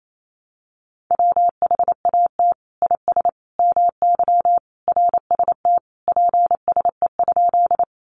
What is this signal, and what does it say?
Morse code, heard as a keyed tone, about 28 wpm, 700 Hz. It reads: WHAT IS MY RST PSE?
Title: Common QSO Phrases 28wpm